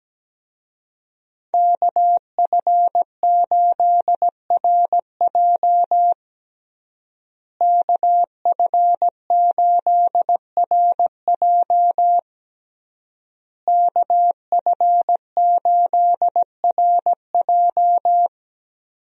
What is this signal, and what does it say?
Morse code, heard as a keyed tone, about 17 wpm, 700 Hz. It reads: KF8RJ KF8RJ KF8RJ
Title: USA Call Signs - 5 Characters - Repeated 3x - 17wpm